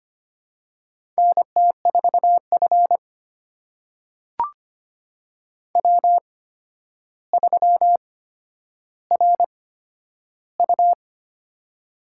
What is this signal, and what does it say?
Morse code, heard as a keyed tone, about 25 wpm, spaced as for 11 wpm, 700 Hz. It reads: NT4F W3RU
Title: USA Call Signs - 4 Characters - 8x Character Spacing - 25wpm